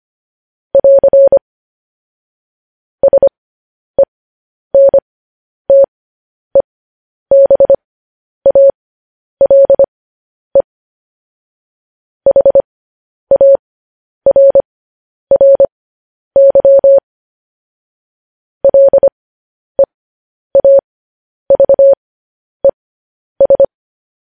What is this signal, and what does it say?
Morse code, heard as a keyed tone, about 25 wpm, spaced as for 10 wpm, 550 Hz. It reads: <AR> SENTEBALE HARRY LEAVES
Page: news-headlines-in-morse-code-at-10-wpm